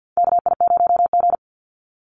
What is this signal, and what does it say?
Morse code, heard as a keyed tone, about 50 wpm, 700 Hz. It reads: KI0G
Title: NA SOTA Chaser Callsigns 50wpm